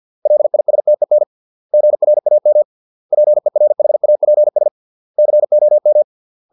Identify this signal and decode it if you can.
Morse code, heard as a keyed tone, about 50 wpm, 600 Hz. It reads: LISTEN GRAM PERHAPS COM